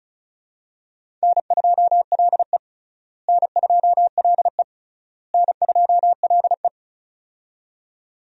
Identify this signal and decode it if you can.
Morse code, heard as a keyed tone, about 34 wpm, 700 Hz. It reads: N2LE N2LE N2LE
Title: USA Call Signs - 4 Characters - Repeated 3x - 35wpm